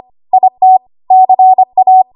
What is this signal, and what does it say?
Morse code, heard as a keyed tone, about 25 wpm, 750 Hz. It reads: IT CA
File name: Archive / 25 WPM / 25WPM